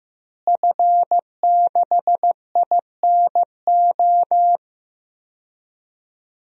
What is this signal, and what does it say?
Morse code, heard as a keyed tone, about 15 wpm, 700 Hz. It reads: F6INO
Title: DXCC-100 Call Signs - 5 Characters - Repeated 2x - 15wpm